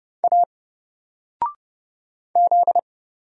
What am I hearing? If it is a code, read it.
Morse code, heard as a keyed tone, about 30 wpm, 700 Hz. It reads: A Z
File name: Single-Letter-Number-30wpm